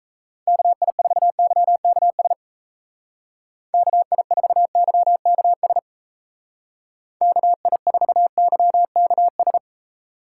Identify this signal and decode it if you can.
Morse code, heard as a keyed tone, about 35 wpm, 700 Hz. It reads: KI4YKS KI4YKS KI4YKS